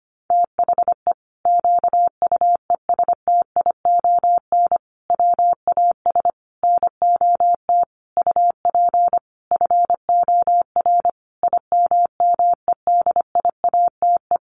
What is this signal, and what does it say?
Morse code, heard as a keyed tone, about 25 wpm, 700 Hz. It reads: THE QUESTION WAS NOT UP FOR IMMEDIATE
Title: Sentences from Top 1200 Words 25wpm